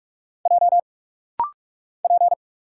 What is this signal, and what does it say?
Morse code, heard as a keyed tone, about 46 wpm, 700 Hz. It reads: J P
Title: Single Letters - Rapid-Fire 45wpm